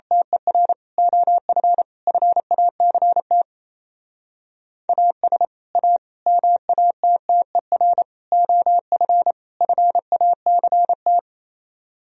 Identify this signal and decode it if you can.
Morse code, heard as a keyed tone, about 29 wpm, 700 Hz. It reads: TER OF FACT AS A MATTER OF FACT
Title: Speed-Racing - Sets of 5 Words 28wpm